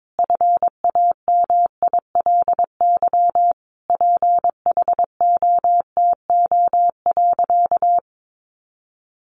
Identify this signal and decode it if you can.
Morse code, heard as a keyed tone, about 22 wpm, 700 Hz. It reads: FAMILY PHOTO.
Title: Sentences from Top 1200 Words 22wpm